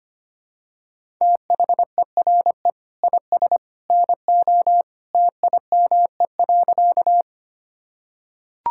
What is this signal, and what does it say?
Morse code, heard as a keyed tone, about 25 wpm, 700 Hz. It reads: THERE IS NO TIME.
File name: practice / 25wpm / Sentences with 2-Word Phrases - 25wpm